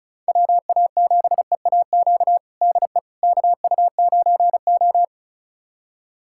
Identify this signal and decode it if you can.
Morse code, heard as a keyed tone, about 36 wpm, 700 Hz. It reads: WA7EAQ DE KU9O
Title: USA Callsign DE Callsign 35wpm